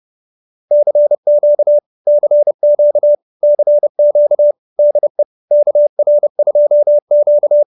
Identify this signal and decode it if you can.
Morse code, heard as a keyed tone, about 30 wpm, 590 Hz. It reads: CQ CQ CQ DE KR2Q